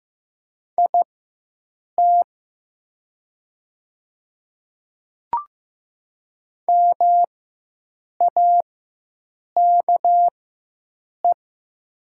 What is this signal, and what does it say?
Morse code, heard as a keyed tone, about 15 wpm, spaced as for 7 wpm, 700 Hz. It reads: IT MAKE
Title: 4x Character Spacing - Top 100 Words - 15wpm